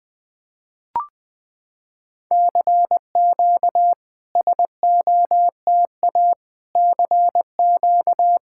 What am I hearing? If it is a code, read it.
Morse code, heard as a keyed tone, about 20 wpm, 700 Hz. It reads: CQ SOTA CQ
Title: SOTA QSO Practice 20wpm